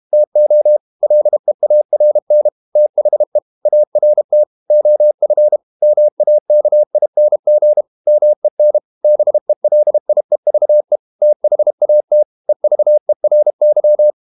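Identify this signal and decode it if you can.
Morse code, heard as a keyed tone, about 32 wpm, 600 Hz. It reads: TO LEARN THE ART OF MAKING MEN BELIEVE THAT EVERY
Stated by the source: Morse code practice at 32 wpm from CW med Gnister, reading Ken Follett's novel Never in English.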